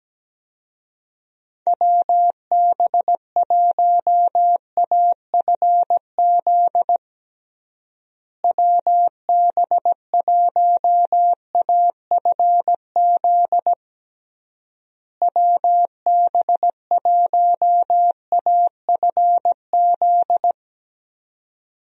Morse code, heard as a keyed tone, about 17 wpm, 700 Hz. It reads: WB1AFZ WB1AFZ WB1AFZ
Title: USA Call Signs - 6 Characters - Repeated 3x - 17wpm